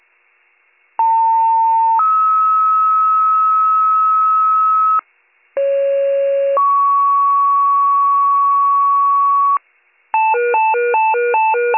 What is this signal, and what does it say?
Two-tone page, heard as a one-second tone, then a three-second tone, about 900 then 1290 Hz, about 550 then 1120 Hz, with noise. It is heard from 1.0 to 9.6 s.